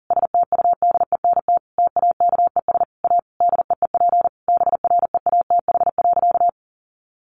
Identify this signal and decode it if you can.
Morse code, heard as a keyed tone, about 40 wpm, 700 Hz. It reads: STUDENT TAKES A DEEP BREATH.
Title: Additional Easy Phrases 40wpm